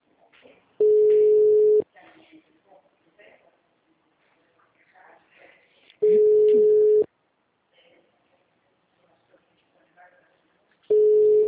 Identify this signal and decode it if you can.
Ringback tone (Europe, 425 Hz), with speech.